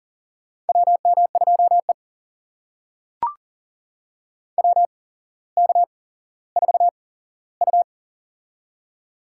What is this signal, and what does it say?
Morse code, heard as a keyed tone, about 40 wpm, spaced as for 17 wpm, 700 Hz. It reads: WM2E WK4U
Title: USA Call Signs - 4 Characters - 8x Character Spacing - 40wpm